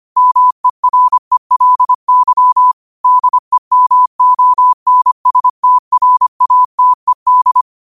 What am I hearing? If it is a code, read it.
Morse code, heard as a keyed tone, about 25 wpm, 1000 Hz. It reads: MERELY DEMONSTRATED
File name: qotdfiles / 25wpm